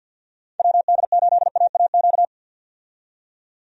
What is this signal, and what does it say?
Morse code, heard as a keyed tone, about 51 wpm, 700 Hz. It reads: WD8AAQ
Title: USA Call Signs - 6 Characters - 50wpm